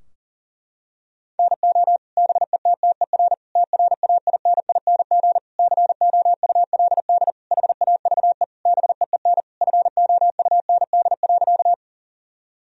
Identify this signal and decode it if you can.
Morse code, heard as a keyed tone, about 37 wpm, 700 Hz. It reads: NO BETTER TRAINING COULD HAVE BEEN FOUND.
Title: Sentences from Top 1000 Words - Review Part 4 - 40wpm